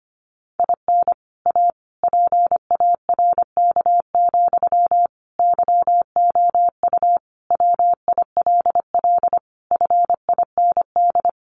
Morse code, heard as a keyed tone, about 25 wpm, 700 Hz. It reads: IN A PARK, YOU WILL FIND
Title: Sentences from Top 1300 Words 25wpm